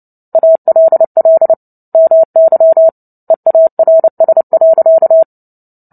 Morse code, heard as a keyed tone, about 30 wpm, 650 Hz. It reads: ALL MY EARS.
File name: Book Two / 03-Chapter 3 - 29 WPM